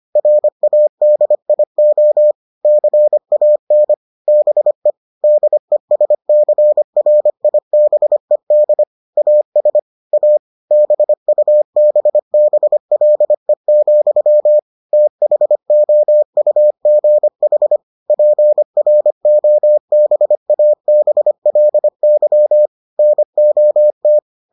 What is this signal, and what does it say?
Morse code, heard as a keyed tone, about 25 wpm, 600 Hz. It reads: RADIO CAN BE DESCRIBED AS A BUBBLE, THOUGH PROBABLY NOT